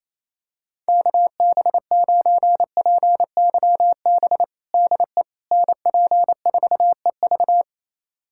Morse code, heard as a keyed tone, about 28 wpm, 700 Hz. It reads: KB9PYB DE NP4EV
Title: USA Callsign DE Callsign 28wpm